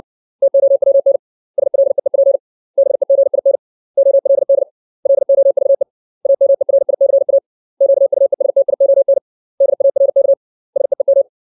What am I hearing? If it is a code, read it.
Morse code, heard as a keyed tone, about 60 wpm, 550 Hz. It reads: TOWN SLEEP BEGIN OLD LOVE AMERICA QUESTION DARK SEEM
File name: 632 words / 632 Words 60WPM